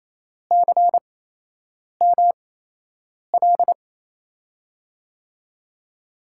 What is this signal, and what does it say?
Morse code, heard as a keyed tone, about 28 wpm, spaced as for 8 wpm, 700 Hz. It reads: CML
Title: USA Call Signs - 5 Characters - 8x Character Spacing - 28wpm